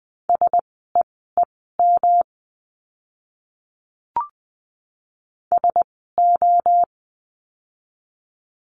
Morse code, heard as a keyed tone, about 20 wpm, spaced as for 14 wpm, 700 Hz. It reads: SEEM SO
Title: 230 Lesson 37f - Letter x - Words 20wpm